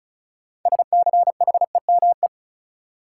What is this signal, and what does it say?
Morse code, heard as a keyed tone, about 34 wpm, 700 Hz. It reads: SCHEME
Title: Six Letter Words - 35wpm